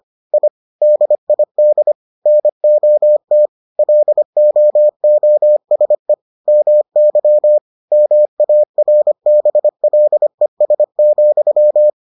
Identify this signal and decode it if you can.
Morse code, heard as a keyed tone, about 25 wpm, 600 Hz. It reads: I DID NOT LOOSE MY MARBLES,